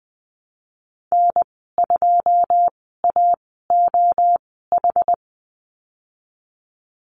Morse code, heard as a keyed tone, about 20 wpm, spaced as for 14 wpm, 700 Hz. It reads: N2AOH
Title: 088 Lesson 15g - Number 5 - Call Signs 20wpm